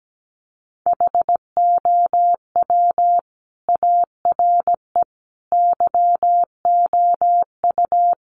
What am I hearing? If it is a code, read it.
Morse code, heard as a keyed tone, about 17 wpm, 700 Hz. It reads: HOW ARE YOU
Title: Sets of 5 Words 17wpm